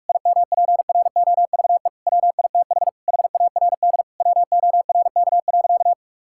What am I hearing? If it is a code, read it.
Morse code, heard as a keyed tone, about 42 wpm, 700 Hz. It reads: IMPROVE WITH HARD WORK.
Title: Sentences from Top 1000 Words - Review Part 5 - 45wpm